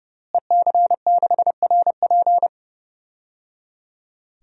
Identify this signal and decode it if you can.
Morse code, heard as a keyed tone, about 30 wpm, 700 Hz. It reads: EC6RP
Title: DXCC-100 Call Signs - 5 Characters - Repeated 3x - 30wpm